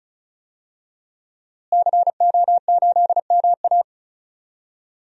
Morse code, heard as a keyed tone, about 36 wpm, 700 Hz. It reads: CO8MA
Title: DXCC-100 Call Signs - 5 Characters - Repeated 2x - 35wpm